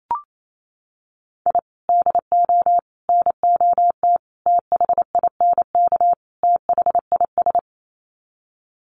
Morse code, heard as a keyed tone, about 28 wpm, 700 Hz. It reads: I DO NOT THINK THIS
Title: Sets of 5 Words Encore 28wpm